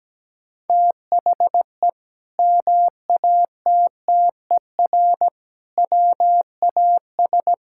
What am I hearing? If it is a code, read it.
Morse code, heard as a keyed tone, about 17 wpm, 700 Hz. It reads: THE MATTER WAS
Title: Sentences from Top 1000 Words - Review Part 5 - 17wpm